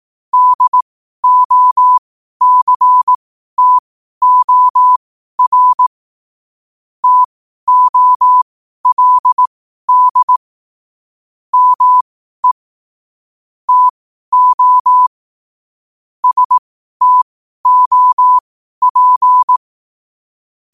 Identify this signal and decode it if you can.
Morse code, heard as a keyed tone, about 18 wpm, spaced as for 12 wpm, 1000 Hz. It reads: DOCTOR TOLD ME TO STOP
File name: qotdfiles / 12wpm